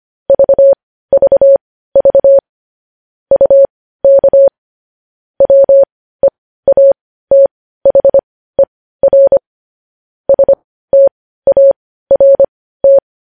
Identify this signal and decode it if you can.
Morse code, heard as a keyed tone, about 25 wpm, spaced as for 15 wpm, 550 Hz. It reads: VVV UK WEATHER START
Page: news headlines in morse code at 15 wpm